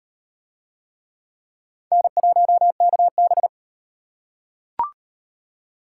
Morse code, heard as a keyed tone, about 36 wpm, 700 Hz. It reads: N1KB